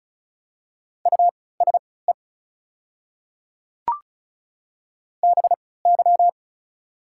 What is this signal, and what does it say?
Morse code, heard as a keyed tone, about 36 wpm, spaced as for 20 wpm, 700 Hz. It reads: USE BY
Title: 3x Character Spacing - Top 500 Words - 35wpm